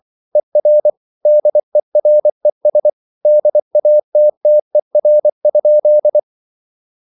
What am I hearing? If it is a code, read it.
Morse code, heard as a keyed tone, about 24 wpm, 600 Hz. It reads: ER DERES DATTER?
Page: Ferie 24 wpm